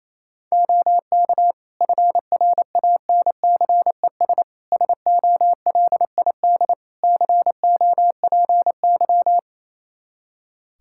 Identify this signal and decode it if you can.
Morse code, heard as a keyed tone, about 28 wpm, 700 Hz. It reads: OK FRANCES SOLID COPY
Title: Practice QSO - Exchange 3 - More Info - 28wpm